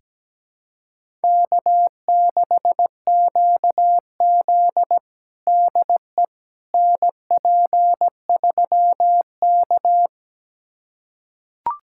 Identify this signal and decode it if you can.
Morse code, heard as a keyed tone, about 17 wpm, 700 Hz. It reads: K6QZ DE NP3K
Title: USA Callsign DE Callsign - 4 Characters - 17wpm